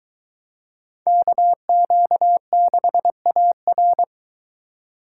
Morse code, heard as a keyed tone, about 21 wpm, 700 Hz. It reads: KQ6AR